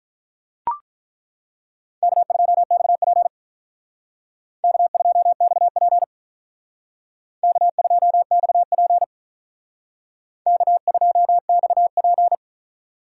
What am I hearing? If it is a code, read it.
Morse code, heard as a keyed tone, about 41 wpm, 700 Hz. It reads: K2XP K2XP K2XP K2XP
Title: Speed Racing - USA Call Signs 35wpm